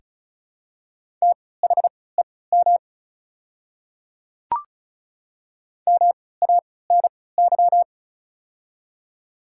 Morse code, heard as a keyed tone, about 36 wpm, spaced as for 20 wpm, 700 Hz. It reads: THEM MANY
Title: 3x Character Spacing - Top 100 Words - 35wpm